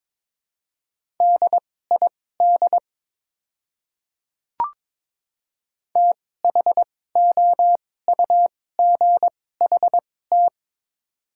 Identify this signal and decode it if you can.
Morse code, heard as a keyed tone, about 22 wpm, spaced as for 16 wpm, 700 Hz. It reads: DID THOUGHT
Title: 2x Character Spacing - Top 300 Words - 22wpm